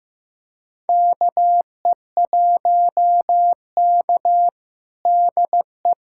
Morse code, heard as a keyed tone, about 16 wpm, 700 Hz. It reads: KE1K DE
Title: USA Callsign DE Callsign - 4 Characters - 15wpm